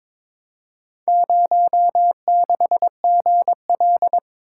Morse code, heard as a keyed tone, about 22 wpm, 700 Hz. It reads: 06GL
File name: practice / 22wpm / Four Letter-Number Combinations - 22wpm